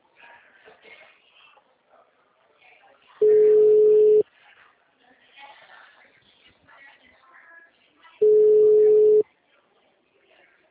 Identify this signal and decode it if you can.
Ringback tone (Europe, 425 Hz), with speech.